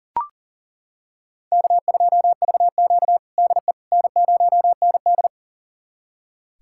Morse code, heard as a keyed tone, about 40 wpm, 700 Hz. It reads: K2VQ DE N0ND
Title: USA Callsign DE Callsign - 4 Characters - 40wpm